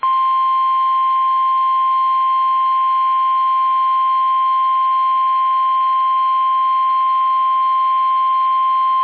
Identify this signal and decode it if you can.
Emergency Alert System (SAME) tones (the 1050 Hz attention tone). It is heard from 0.7 to 8.6 s.
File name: eas / KUSO / KUSO